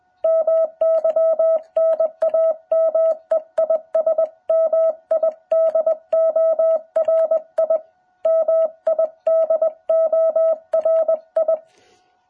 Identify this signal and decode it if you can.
Morse code, heard as a keyed tone, about 20 wpm, 660 Hz. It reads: MY NAME IS MIDORI MIDORI
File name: jh8cbh / 2021cw